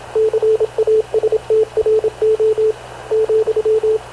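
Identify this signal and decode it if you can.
Morse code, heard as a keyed tone, about 27 wpm, 440 Hz, with radio noise. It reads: CASTRO,
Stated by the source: Havana Radio CLA press (PX) broadcast, recorded by Radio Officer Paul Dumesnil aboard mv CNAV QUEST/CZDO.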